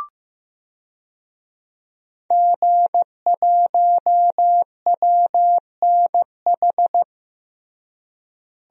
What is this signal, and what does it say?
Morse code, heard as a keyed tone, about 15 wpm, 700 Hz. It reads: G1WNH
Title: DXCC-100 Call Signs - 5 Characters - Repeated 3x - 15wpm